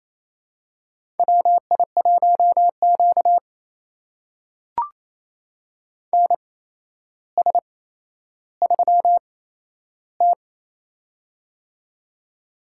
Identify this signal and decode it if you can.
Morse code, heard as a keyed tone, about 28 wpm, spaced as for 12 wpm, 700 Hz. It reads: WI1Q NS3T
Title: USA Call Signs - 4 Characters - 8x Character Spacing - 28wpm